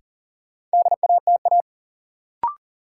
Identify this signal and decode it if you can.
Morse code, heard as a keyed tone, about 40 wpm, 700 Hz. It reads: DATA
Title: Sentences from Top 1000 Words - Review Part 3 - New Word Drill - 40wpm